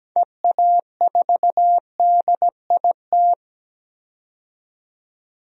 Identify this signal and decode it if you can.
Morse code, heard as a keyed tone, about 17 wpm, 700 Hz. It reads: EA4DIT
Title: DXCC-100 Call Signs - 6 Characters - 17wpm